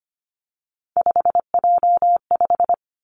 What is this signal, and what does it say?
Morse code, heard as a keyed tone, about 25 wpm, 700 Hz. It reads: 5J5
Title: Three Challenging Letter-Number Combinations 25wpm